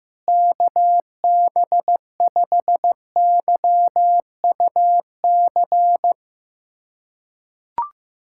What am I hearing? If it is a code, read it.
Morse code, heard as a keyed tone, about 15 wpm, 700 Hz. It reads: KB5YUC
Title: USA Callsign DE Callsign 15wpm